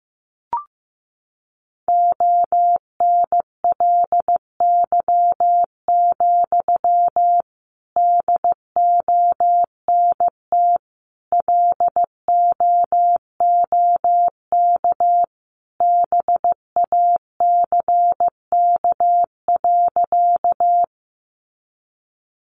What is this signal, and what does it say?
Morse code, heard as a keyed tone, about 14 wpm, 700 Hz. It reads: ONLY, DONT LOOK BACK.
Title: Sentences from Top 300 Words 15wpm